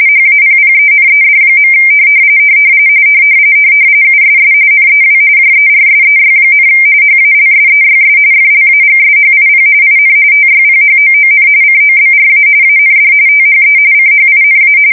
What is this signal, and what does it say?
RTTY, about 45.45 baud. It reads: HAPPENED. USS NEW JERSEY, HALSEYS FLAGSHIP IN TYPHOON COBRA AND THE CHINA
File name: fsk170 / battleshipnj